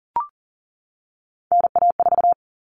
Morse code, heard as a keyed tone, about 40 wpm, 700 Hz. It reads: NA4
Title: All Cut Numbers - 3-Digit 40wpm